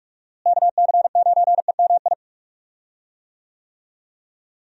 Morse code, heard as a keyed tone, about 46 wpm, 700 Hz. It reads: KC9EMI
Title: USA Call Signs - 6 Characters - Repeated 2x - 45wpm